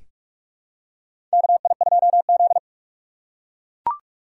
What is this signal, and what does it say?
Morse code, heard as a keyed tone, about 43 wpm, 700 Hz. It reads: KI2Z
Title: USA Call Signs - 4 Characters - 6x Character Spacing - 45wpm